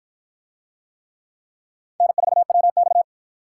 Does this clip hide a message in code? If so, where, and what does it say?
Yes: Morse code from 2.0 to 3.0 s, heard as a keyed tone, about 48 wpm, 700 Hz. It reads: N4WX